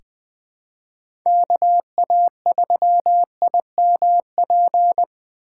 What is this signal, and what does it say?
Morse code, heard as a keyed tone, about 20 wpm, 700 Hz. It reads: KA3IMP
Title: USA Call Signs - 6 Characters - 2x Character Spacing - 20wpm